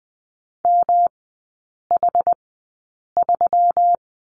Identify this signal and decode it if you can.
Morse code, heard as a keyed tone, about 20 wpm, spaced as for 8 wpm, 700 Hz. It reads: MH3
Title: 178 Lesson 29e - Slash - Three Characters 20wpm